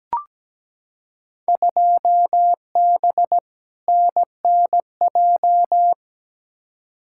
Morse code, heard as a keyed tone, about 17 wpm, 700 Hz. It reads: 2B NNJ
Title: Field Day Exchanges Encore - 17wpm